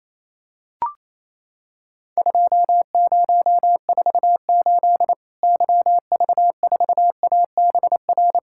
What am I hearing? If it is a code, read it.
Morse code, heard as a keyed tone, about 28 wpm, 700 Hz. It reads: 2048 YV4ABR
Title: Worked All Europe DX Contest - QTC Exchange - 28wpm